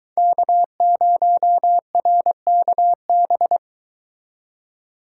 Morse code, heard as a keyed tone, about 23 wpm, 700 Hz. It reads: K0RKB